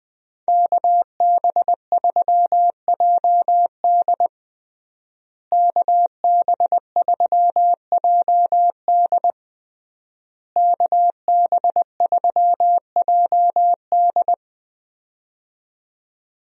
Morse code, heard as a keyed tone, about 21 wpm, 700 Hz. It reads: KB3JD KB3JD KB3JD
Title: USA Call Signs - 5 Characters - Repeated 3x - 20wpm